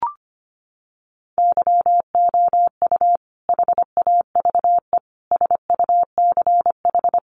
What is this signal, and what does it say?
Morse code, heard as a keyed tone, about 25 wpm, 700 Hz. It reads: YOU HAVE SUCH
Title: Sentences from Top 1200 Words 25wpm